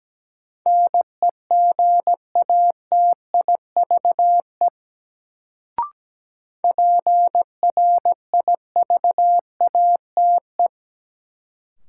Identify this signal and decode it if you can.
Morse code, heard as a keyed tone, about 17 wpm, 700 Hz. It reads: NEGATIVE PRIVATE
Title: Sentences from Top 1000 Words - New Word Drill - 17wpm